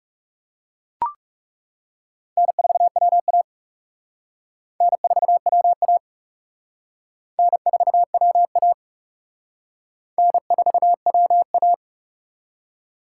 Morse code, heard as a keyed tone, about 36 wpm, 700 Hz. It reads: N4WA N4WA N4WA N4WA
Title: Speed Racing - USA 1x2 Call Signs 30wpm